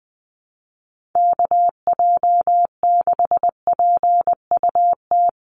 Morse code, heard as a keyed tone, about 20 wpm, 700 Hz. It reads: KJ6PUT
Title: USA Call Signs 20wpm